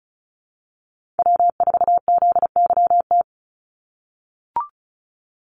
Morse code, heard as a keyed tone, about 34 wpm, 700 Hz. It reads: W4ZYT
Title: Call Signs - 35wpm